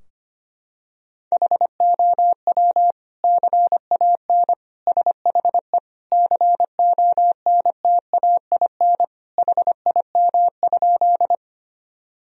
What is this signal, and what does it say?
Morse code, heard as a keyed tone, about 25 wpm, 700 Hz. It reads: HOW CAN SHE CONTAIN HIM?
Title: Sentences from Top 1000 Words - Review Part 2 - 25wpm